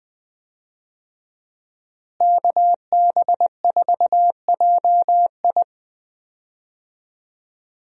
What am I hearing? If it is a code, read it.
Morse code, heard as a keyed tone, about 20 wpm, 700 Hz. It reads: KB4JI
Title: USA Call Signs - 5 Characters - Repeated 2x - 20wpm